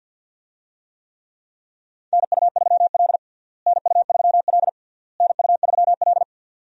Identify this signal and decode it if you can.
Morse code, heard as a keyed tone, about 47 wpm, 700 Hz. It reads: NU3L NU3L NU3L
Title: USA Call Signs - 4 Characters - Repeated 3x - 50wpm